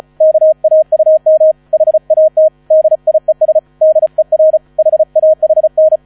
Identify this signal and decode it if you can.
Morse code, heard as a keyed tone, about 34 wpm, 630 Hz. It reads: KAUM HAT DIES DER HAHN
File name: 35wpm